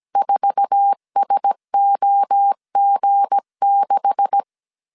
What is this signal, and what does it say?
Morse code, heard as a keyed tone, about 17 wpm, 800 Hz. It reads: 4SOG6
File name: CW / DET / 1 9 1500C S C 16wpm 2909